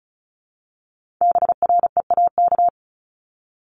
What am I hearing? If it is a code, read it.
Morse code, heard as a keyed tone, about 36 wpm, 700 Hz. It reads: BREAK
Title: Multi-letter Phonemes Repeated 3x 35wpm